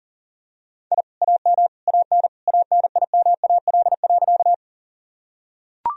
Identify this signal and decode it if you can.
Morse code, heard as a keyed tone, about 40 wpm, 700 Hz. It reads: I AM AN ANIMAL.
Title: Sentences from Top 1000 Words - Review Part 1 - 40wpm